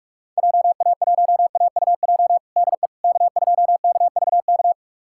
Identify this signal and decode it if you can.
Morse code, heard as a keyed tone, about 46 wpm, 690 Hz. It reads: JA1AUJ DE K2KVK
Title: USA Callsign DE Callsign 45wpm